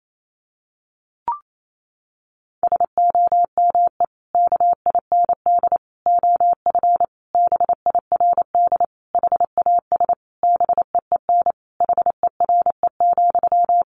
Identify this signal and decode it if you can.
Morse code, heard as a keyed tone, about 28 wpm, 700 Hz. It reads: SOME KIND OF BIRD HAS BEEN HERE,
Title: Sentences from Top 1200 Words 28wpm